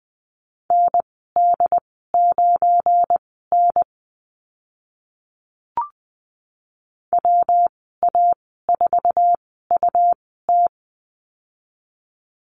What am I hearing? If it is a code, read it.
Morse code, heard as a keyed tone, about 20 wpm, spaced as for 15 wpm, 700 Hz. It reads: ND9N WA4UT
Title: 186 Lesson 30g - Letter b - Call Signs 20wpm